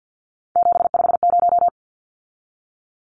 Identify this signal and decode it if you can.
Morse code, heard as a keyed tone, about 50 wpm, 700 Hz. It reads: 750
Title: Numbers - 3-Digit 50wpm